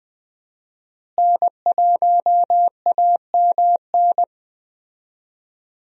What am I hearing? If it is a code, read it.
Morse code, heard as a keyed tone, about 20 wpm, 700 Hz. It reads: N1AMN
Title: USA Call Signs - 5 Characters - Repeated 3x - 20wpm